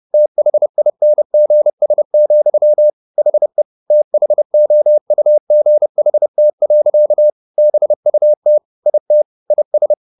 Morse code, heard as a keyed tone, about 30 wpm, 600 Hz. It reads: THINGS, HE THOUGHT. BUT IT IS